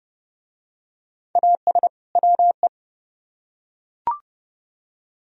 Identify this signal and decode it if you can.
Morse code, heard as a keyed tone, about 30 wpm, 700 Hz. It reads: AS WE